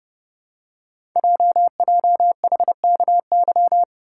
Morse code, heard as a keyed tone, about 30 wpm, 690 Hz. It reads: JJHKY
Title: Code Groups - 5 Letters - 30wpm